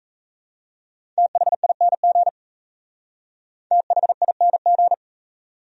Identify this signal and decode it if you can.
Morse code, heard as a keyed tone, about 35 wpm, 700 Hz. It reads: THING THING
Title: Speed-Racing Top 100 Words 28wpm